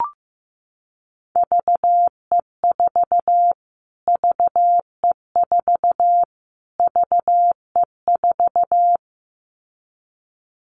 Morse code, heard as a keyed tone, about 15 wpm, 700 Hz. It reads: VE4 VE4 VE4
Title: Callsign Prefixes for Top 25 DXCC - Repeated 3x - 15wpm